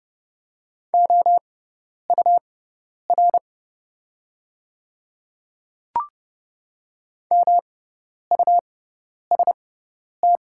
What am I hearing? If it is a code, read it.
Morse code, heard as a keyed tone, about 30 wpm, spaced as for 10 wpm, 700 Hz. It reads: OUR MUST
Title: 6x Character Spacing - Top 300 Words - 30wpm